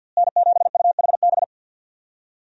Morse code, heard as a keyed tone, about 51 wpm, 700 Hz. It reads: N7UHB
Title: USA Call Signs - 5 Characters - Repeated 3x - 50wpm